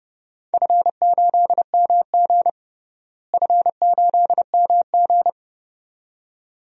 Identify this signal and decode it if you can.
Morse code, heard as a keyed tone, about 30 wpm, 700 Hz. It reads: F8MG F8MG
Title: DXCC-100 Call Signs - 4 Characters - Repeated 2x - 30wpm